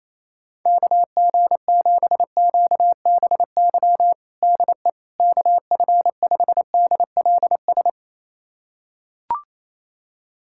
Morse code, heard as a keyed tone, about 28 wpm, 700 Hz. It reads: KG7QBY DE KF5DLS
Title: USA Callsign DE Callsign - 6 Characters - 28wpm